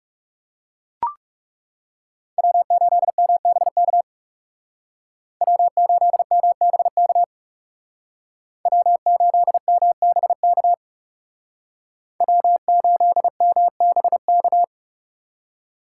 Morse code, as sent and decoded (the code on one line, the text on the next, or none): .-- ---.. -- -... -.- / .-- ---.. -- -... -.- / .-- ---.. -- -... -.- / .-- ---.. -- -... -.-
W8MBK W8MBK W8MBK W8MBK